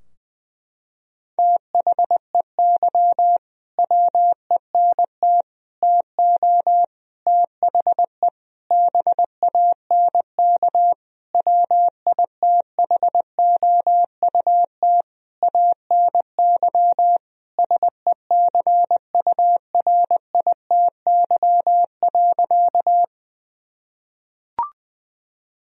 Morse code, heard as a keyed tone, about 20 wpm, 700 Hz. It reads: THEY WENT TO THE BANK WITHOUT ANY SECURITY.
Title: Sentences from Top 1000 Words - Review Part 4 - 20wpm